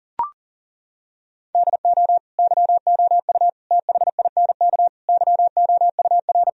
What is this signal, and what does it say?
Morse code, heard as a keyed tone, about 40 wpm, 700 Hz. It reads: DO YOU THINK YOUR
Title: Sentences from Top 1000 Words - Review Part 1 - 40wpm